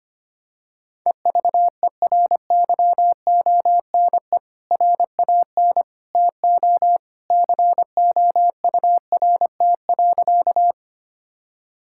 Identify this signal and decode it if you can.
Morse code, heard as a keyed tone, about 25 wpm, 700 Hz. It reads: EVERYONE RAN TO COURT.
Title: Sentences from Top 1600 Words 25wpm